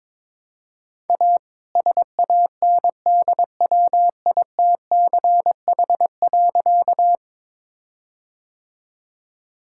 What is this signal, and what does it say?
Morse code, heard as a keyed tone, about 22 wpm, 700 Hz. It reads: A SANDWITCH.
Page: morse-code-ninja-practice-22wpm